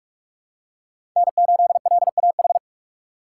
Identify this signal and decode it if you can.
Morse code, heard as a keyed tone, about 43 wpm, 700 Hz. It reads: N8LAH
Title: USA Call Signs - 6x Character Spacing - 45wpm